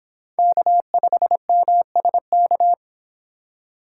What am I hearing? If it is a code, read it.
Morse code, heard as a keyed tone, about 26 wpm, 700 Hz. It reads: K5MSK
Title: Speed Racing - USA Call Signs 22wpm